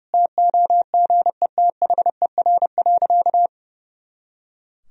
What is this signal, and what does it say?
Morse code, heard as a keyed tone, about 30 wpm, 700 Hz. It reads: TOGETHER.
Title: Sentences from Top 500 Words and Abbreviations 30wpm